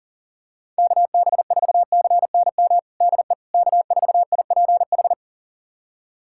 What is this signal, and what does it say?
Morse code, heard as a keyed tone, about 40 wpm, 700 Hz. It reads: KB4CNM DE K4IPH
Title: USA Callsign DE Callsign 40wpm